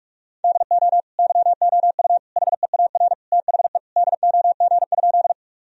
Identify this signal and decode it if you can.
Morse code, heard as a keyed tone, about 46 wpm, 700 Hz. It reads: DO YOU HEAR THE DOG?
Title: Sentences from Top 1000 Words - Review Part 2 - 45wpm